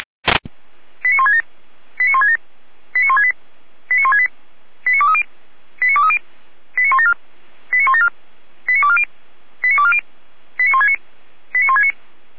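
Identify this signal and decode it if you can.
Five-tone selective call (ZVEI), heard as quick runs of five tones, tones of about 70 ms. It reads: 89167 89167 89168 89168 89230 89230 89174 89174 89240 89240 89169 89169
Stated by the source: ZVEI five-tone selective call received on SDR.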